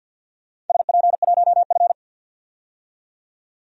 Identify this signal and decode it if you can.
Morse code, heard as a keyed tone, about 52 wpm, 700 Hz. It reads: SP1F
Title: DXCC-100 Call Signs - 4 Characters - 50wpm